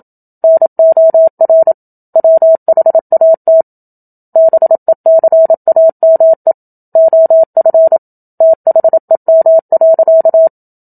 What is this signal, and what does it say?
Morse code, heard as a keyed tone, about 27 wpm, 650 Hz. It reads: NOR WHAT BECAME OF THEM.
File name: Book Two / 01-Chapter 1 - 27 WPM